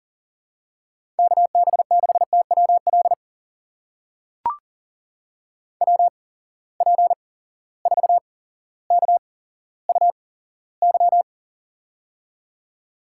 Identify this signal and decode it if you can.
Morse code, heard as a keyed tone, about 40 wpm, spaced as for 17 wpm, 700 Hz. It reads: KB6TWL WP4KUY
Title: USA Call Signs - 6 Characters - 8x Character Spacing - 40wpm